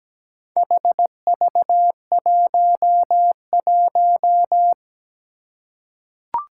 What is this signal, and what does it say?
Morse code, heard as a keyed tone, about 17 wpm, 700 Hz. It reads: HV11
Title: Four Challenging Letter-Number Combinations 17wpm